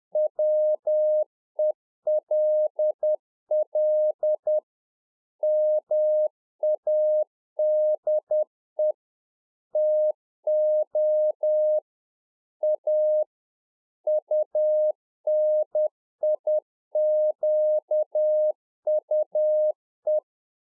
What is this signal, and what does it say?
Morse code, heard as a keyed tone, about 10 wpm, 610 Hz. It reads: WELL MADE TO A UNIQUE
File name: Czech Army Morse Keys 10WPM